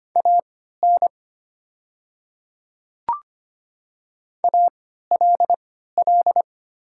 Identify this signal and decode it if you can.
Morse code, heard as a keyed tone, about 25 wpm, spaced as for 14 wpm, 700 Hz. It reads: AN ALL
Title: 3x Character Spacing - Top 100 Words - 25wpm